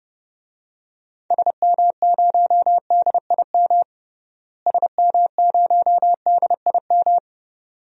Morse code, heard as a keyed tone, about 32 wpm, 700 Hz. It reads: SM0DIM SM0DIM
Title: DXCC-100 Call Signs - 6 Characters - Repeated 2x - 30wpm